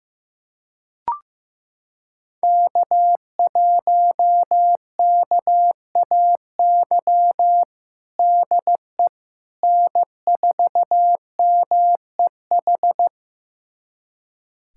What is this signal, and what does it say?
Morse code, heard as a keyed tone, about 15 wpm, 700 Hz. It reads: K1KAY DE N4MEH
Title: USA Callsign DE Callsign - 5 Characters - 15wpm